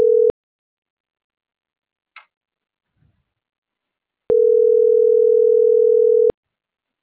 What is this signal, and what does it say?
Ringback tone (North America), 440 + 480 Hz.